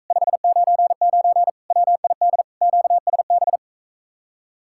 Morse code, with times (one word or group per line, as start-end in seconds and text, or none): 0.1-1.5 599
1.7-2.4 WID
2.6-3.6 QSB